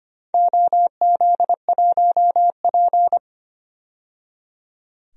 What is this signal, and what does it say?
Morse code, heard as a keyed tone, about 25 wpm, 700 Hz. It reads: OZ1P
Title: DXCC-100 Call Signs - 4 Characters - 25wpm